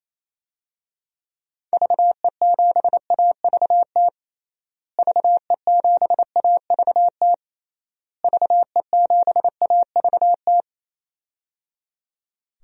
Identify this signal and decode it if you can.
Morse code, heard as a keyed tone, about 28 wpm, 700 Hz. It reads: VE7AVT VE7AVT VE7AVT
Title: DXCC-100 Call Signs - 6 Characters - Repeated 3x - 28wpm